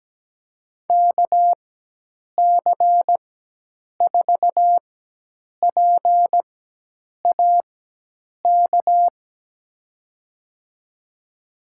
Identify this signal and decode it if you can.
Morse code, heard as a keyed tone, about 17 wpm, spaced as for 8 wpm, 700 Hz. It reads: KC4PAK
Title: USA Call Signs - 6 Characters - 4x Character Spacing - 17wpm